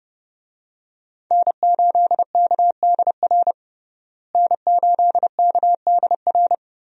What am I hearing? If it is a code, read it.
Morse code, heard as a keyed tone, about 30 wpm, 700 Hz. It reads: N8KDR N8KDR
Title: USA Call Signs - 5 Characters - Repeated 3x - 30wpm